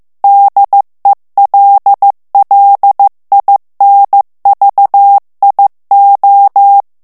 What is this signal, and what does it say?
Morse code, heard as a keyed tone, about 15 wpm, 800 Hz. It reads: DELLINVIO
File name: morse mp3 / ITALIEN texte clair 2 vitesse 14 mots minute